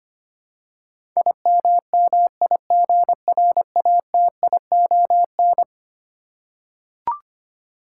Morse code, heard as a keyed tone, about 25 wpm, 700 Hz. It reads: IMMIGRATION
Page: morse-code-ninja-practice-25wpm